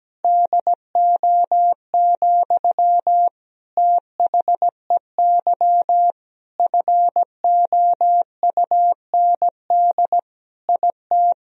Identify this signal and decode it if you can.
Morse code, heard as a keyed tone, about 17 wpm, 700 Hz. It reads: DO, THEY FOUND IT
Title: Sentences from Top 1000 Words - Review Part 4 - 17wpm